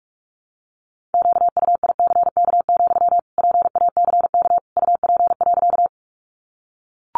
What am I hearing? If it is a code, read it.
Morse code, heard as a keyed tone, about 44 wpm, 700 Hz. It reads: QUICK, PACK UP.
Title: Sentences from Top 1300 Words 45wpm